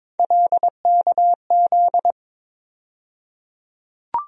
Morse code, heard as a keyed tone, about 21 wpm, 700 Hz. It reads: LKZ
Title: USA Call Sign Postfixes - 3 Letters 22wpm